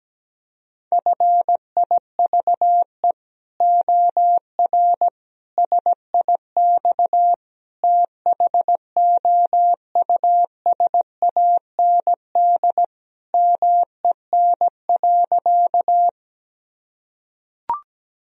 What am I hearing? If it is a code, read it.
Morse code, heard as a keyed tone, about 17 wpm, 700 Hz. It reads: FIVE OR SIX THOUSAND MEN.
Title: Sentences from Top 1000 Words - Review Part 3 - 17wpm